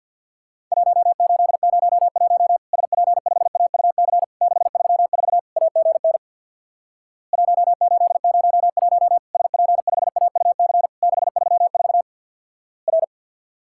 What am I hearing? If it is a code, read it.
Morse code, heard as a keyed tone, about 50 wpm, 700 Hz. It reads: 1801 SP5AUC 634 AGN 1801 SP5AUC 634 R
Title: Worked All Europe DX Contest - QTC Exchange with Repeats - 50wpm